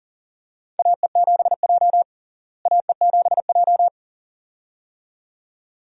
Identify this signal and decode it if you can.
Morse code, heard as a keyed tone, about 40 wpm, 700 Hz. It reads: AE7J AE7J
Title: USA Call Signs - 4 Characters - Repeated 2x - 40wpm